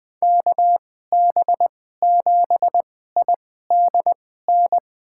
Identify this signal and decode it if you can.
Morse code, heard as a keyed tone, about 20 wpm, spaced as for 14 wpm, 700 Hz. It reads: KB7IDN
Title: USA Call Signs - 6 Characters - 2x Character Spacing - 20wpm